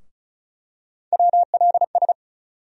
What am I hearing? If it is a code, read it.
Morse code, heard as a keyed tone, about 34 wpm, 700 Hz. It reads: WLS
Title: USA Call Sign Postfixes - 3 Letters 35wpm